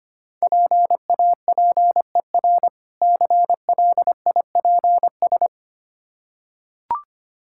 Morse code, heard as a keyed tone, about 25 wpm, 700 Hz. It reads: PAPER CLIPS
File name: practice / 25wpm / Difficult Words with Hints - 25wpm